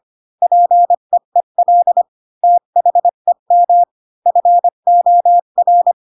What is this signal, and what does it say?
Morse code, heard as a keyed tone, about 25 wpm, 700 Hz. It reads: PEEL THEM FOR
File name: Pinocchio / Pinocchio ch7 20 25 ik1hge